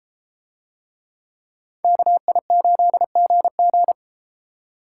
Morse code, heard as a keyed tone, about 32 wpm, 700 Hz. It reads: KI8GG